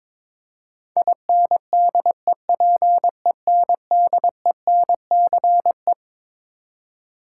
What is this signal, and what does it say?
Morse code, heard as a keyed tone, about 22 wpm, 700 Hz. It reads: INDEPENDENCE
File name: Twelve Letter Words - 22wpm